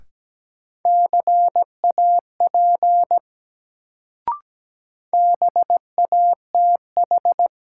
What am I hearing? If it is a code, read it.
Morse code, heard as a keyed tone, about 17 wpm, 700 Hz. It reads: CAP BATH
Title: Sentences from Top 1600 Words - New Word Drill Alt - 17wpm